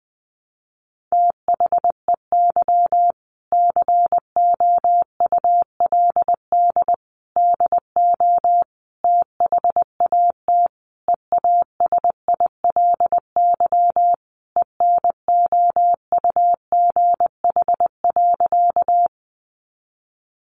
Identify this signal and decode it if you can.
Morse code, heard as a keyed tone, about 20 wpm, 700 Hz. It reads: THEY COULD DO THAT EASILY ENOUGH.
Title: Sentences from Top 900 Words 20wpm